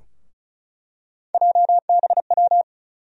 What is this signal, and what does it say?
Morse code, heard as a keyed tone, about 36 wpm, 700 Hz. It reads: JBW
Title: USA Call Sign Postfixes - 3 Letters 35wpm